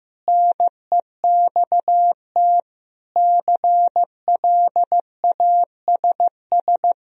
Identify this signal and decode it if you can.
Morse code, heard as a keyed tone, about 15 wpm, 700 Hz. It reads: NEXT CLASS
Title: Sentences from Top 1000 Words - Review Part 2 - 15wpm